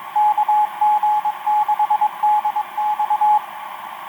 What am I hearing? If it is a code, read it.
Morse code, heard as a keyed tone, about 22 wpm, 840 Hz, with radio noise. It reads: KG6DX
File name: kg6dx om3ey etc 12m cw 14feb2011 0851